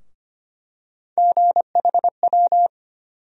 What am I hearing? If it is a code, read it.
Morse code, heard as a keyed tone, about 25 wpm, 700 Hz. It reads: GHW
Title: USA Call Sign Postfixes - 3 Letters 25wpm